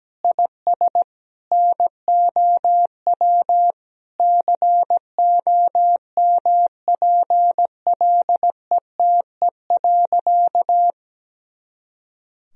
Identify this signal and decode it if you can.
Morse code, heard as a keyed tone, about 17 wpm, 700 Hz. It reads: IS NOW COMPLETE.
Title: Sentences from Top 400 Words and Abbreviations 17wpm